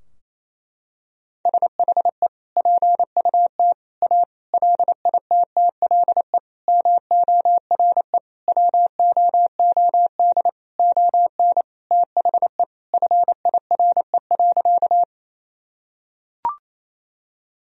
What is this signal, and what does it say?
Morse code, heard as a keyed tone, about 27 wpm, 700 Hz. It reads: SHE PUT A LITTLE MORE WOOD ON THE FIRE.
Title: Sentences from Top 1000 Words - Review Part 2 - 28wpm